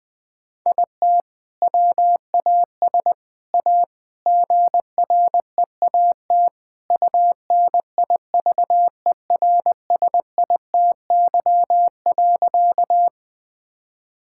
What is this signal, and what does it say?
Morse code, heard as a keyed tone, about 20 wpm, 700 Hz. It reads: IT WAS A GREAT UNIVERSITY.
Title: Sentences from Top 1000 Words - Review Part 5 - 20wpm